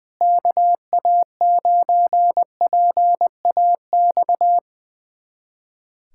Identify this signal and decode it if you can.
Morse code, heard as a keyed tone, about 21 wpm, 700 Hz. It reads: KA9PAX
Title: USA Callsign DE Callsign - 6 Characters - 20wpm